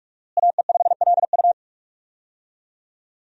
Morse code, heard as a keyed tone, about 45 wpm, 700 Hz. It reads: AE5LU
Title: USA Call Signs - 5 Characters - Repeated 3x - 45wpm